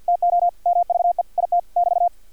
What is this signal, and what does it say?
Morse code, heard as a keyed tone, about 50 wpm, 700 Hz, with radio noise. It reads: TO MVE IT <BT>
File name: archived / CRSnet